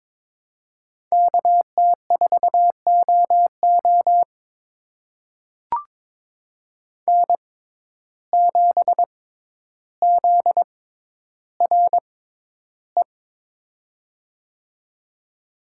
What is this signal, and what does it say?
Morse code, heard as a keyed tone, about 22 wpm, spaced as for 11 wpm, 700 Hz. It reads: KT4OO N7ZRE
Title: USA Call Signs - 5 Characters - 6x Character Spacing - 22wpm